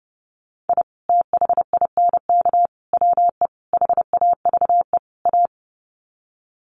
Morse code, heard as a keyed tone, about 30 wpm, 700 Hz. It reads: I THINK WE HAVE A
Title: Sets of 5 Words Encore 30wpm